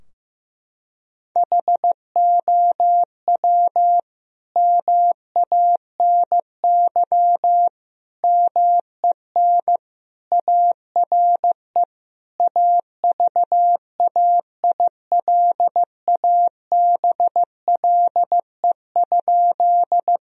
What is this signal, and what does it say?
Morse code, heard as a keyed tone, about 15 wpm, 700 Hz. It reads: HOW MANY MEN ARE AVAILABLE?
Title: Sentences from Top 1000 Words - Review Part 3 - 15wpm